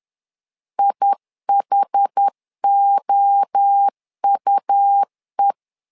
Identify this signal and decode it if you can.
Morse code, heard as a keyed tone, about 10 wpm, 800 Hz. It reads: IHOUE
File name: LEZIONI / 4 Q1 10wpm 2448